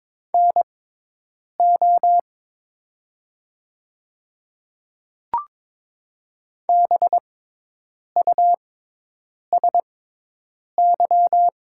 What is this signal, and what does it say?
Morse code, heard as a keyed tone, about 22 wpm, spaced as for 8 wpm, 700 Hz. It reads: NO BUSY
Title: 6x Character Spacing - Top 500 Words - 22wpm